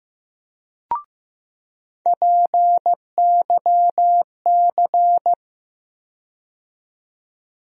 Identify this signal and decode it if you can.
Morse code, heard as a keyed tone, about 15 wpm, 700 Hz. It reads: PYC
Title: USA Call Sign Postfixes - 3 Letters 15wpm